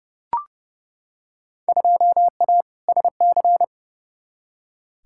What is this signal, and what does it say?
Morse code, heard as a keyed tone, about 30 wpm, 700 Hz. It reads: 2A SC
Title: Field Day Exchanges - 30wpm